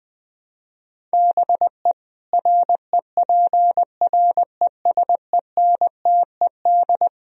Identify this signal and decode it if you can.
Morse code, heard as a keyed tone, about 20 wpm, 700 Hz. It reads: BE REPRESENTED
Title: Speed-Racing - Sets of 5 Words 17wpm